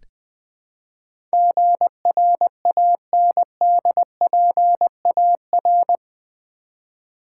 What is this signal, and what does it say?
Morse code, heard as a keyed tone, about 20 wpm, 700 Hz. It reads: GRANDPAR
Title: Build A Word - 4 Letters At A Time - 11-Letter Words - 20wpm